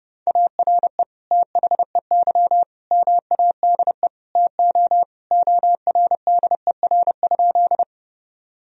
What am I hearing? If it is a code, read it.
Morse code, heard as a keyed tone, about 32 wpm, 700 Hz. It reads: ARE THEY MADE TO ORDER?
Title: Sentences from Top 1000 Words - Review Part 2 - 30wpm